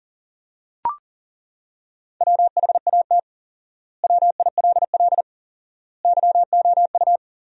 Morse code, heard as a keyed tone, about 40 wpm, 700 Hz. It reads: WHAT WILL YOU
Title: 4x Word-Spacing - Sentences from Top 200 Words - 40wpm